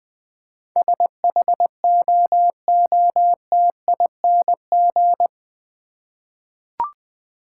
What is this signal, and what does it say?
Morse code, heard as a keyed tone, about 20 wpm, 700 Hz. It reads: SHOOTING
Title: Eight Letter Words - 20wpm